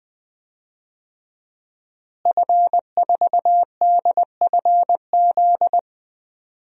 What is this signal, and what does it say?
Morse code, heard as a keyed tone, about 21 wpm, 700 Hz. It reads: F4DFZ
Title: DXCC-100 Call Signs - 5 Characters - Repeated 2x - 20wpm